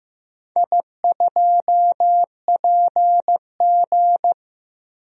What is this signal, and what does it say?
Morse code, heard as a keyed tone, about 15 wpm, 700 Hz. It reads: I2PG
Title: DXCC-100 Call Signs - 4 Characters - 15wpm